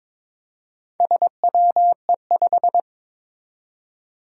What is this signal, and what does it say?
Morse code, heard as a keyed tone, about 22 wpm, 700 Hz. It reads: SWE5
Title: Four Challenging Letter-Number Combinations 22wpm